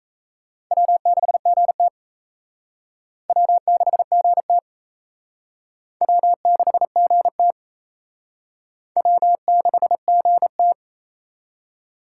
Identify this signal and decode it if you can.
Morse code, heard as a keyed tone, about 34 wpm, 700 Hz. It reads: W6GT W6GT W6GT W6GT
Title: Speed Racing - USA Call Signs 28wpm